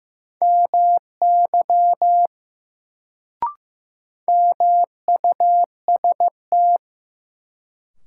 Morse code, heard as a keyed tone, about 16 wpm, 700 Hz. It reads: MY MUST
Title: Sets of 2 Words - New Word Drill - 15wpm